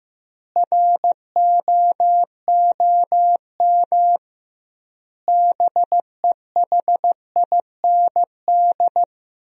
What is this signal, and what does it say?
Morse code, heard as a keyed tone, about 15 wpm, 700 Hz. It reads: ROOM BEHIND
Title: 2x Word-Spacing - Sentences from Top 500 Words - 15wpm